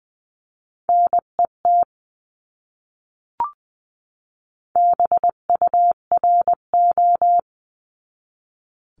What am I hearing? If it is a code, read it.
Morse code, heard as a keyed tone, about 20 wpm, 700 Hz. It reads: NET BURO
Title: 259 Lesson 42d - Words 18-20wpm